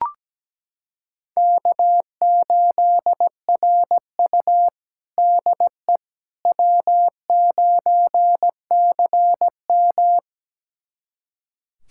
Morse code, heard as a keyed tone, about 17 wpm, 700 Hz. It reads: K8RU DE W9CM
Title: USA Callsign DE Callsign - 4 Characters - 17wpm